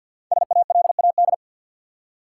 Morse code, heard as a keyed tone, about 52 wpm, 700 Hz. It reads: SALAD
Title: Five Letter Words - 50wpm